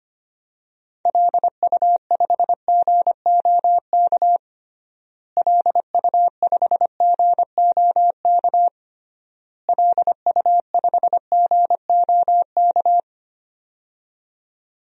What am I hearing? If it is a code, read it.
Morse code, heard as a keyed tone, about 25 wpm, 700 Hz. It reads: LU5GOK LU5GOK LU5GOK
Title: DXCC-100 Call Signs - 6 Characters - Repeated 3x - 25wpm